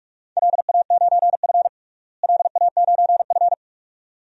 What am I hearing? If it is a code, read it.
Morse code, heard as a keyed tone, about 42 wpm, 700 Hz. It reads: LA9F LA9F
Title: DXCC-100 Call Signs - 4 Characters - Repeated 2x - 45wpm